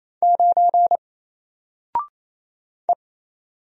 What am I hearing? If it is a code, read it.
Morse code, heard as a keyed tone, about 27 wpm, 700 Hz. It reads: 9 E
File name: Single-Letter-Number-28wpm